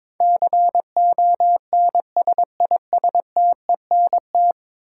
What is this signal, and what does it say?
Morse code, heard as a keyed tone, about 22 wpm, 700 Hz. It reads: CONSISTENT
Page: morse-code-ninja-practice-22wpm